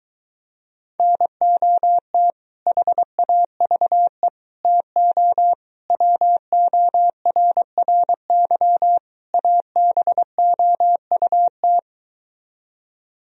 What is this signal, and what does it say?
Morse code, heard as a keyed tone, about 23 wpm, 700 Hz. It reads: NOT HAVE TO WORRY ABOUT